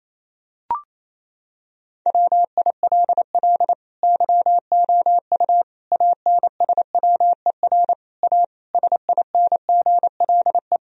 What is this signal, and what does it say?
Morse code, heard as a keyed tone, about 28 wpm, 700 Hz. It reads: WILL YOU ANSWER A SINGLE
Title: Sentences from Top 1000 Words - Review Part 4 - 28wpm